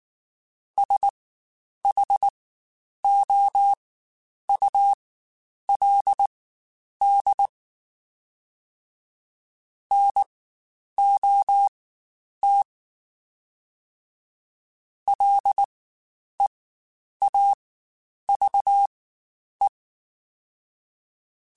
Morse code, heard as a keyed tone, about 19 wpm, spaced as for 9 wpm, 800 Hz. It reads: SHOULD NOT LEAVE
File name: Spaced / Chuckles Spaced-mp3s / Temptation sp @18